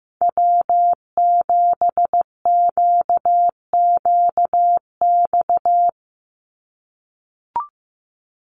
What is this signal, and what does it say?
Morse code, heard as a keyed tone, about 15 wpm, 700 Hz. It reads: W7QQX
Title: USA Call Signs 15wpm